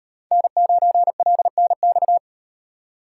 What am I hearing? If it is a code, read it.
Morse code, heard as a keyed tone, about 35 wpm, 700 Hz. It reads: N9LNX